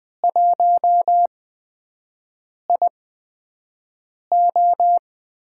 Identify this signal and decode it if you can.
Morse code, heard as a keyed tone, about 20 wpm, spaced as for 5 wpm, 700 Hz. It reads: 1IO
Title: USA Call Signs - 6 Characters - 8x Character Spacing - 20wpm